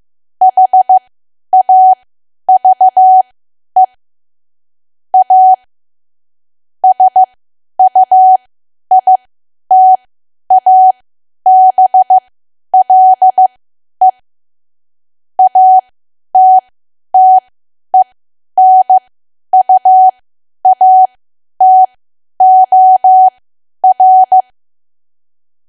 Morse code, heard as a keyed tone, about 15 wpm, spaced as for 10 wpm, 750 Hz. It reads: HAVE A SUITABLE ATTENUATOR
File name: cwa / W1AW-Level2 / 10WPM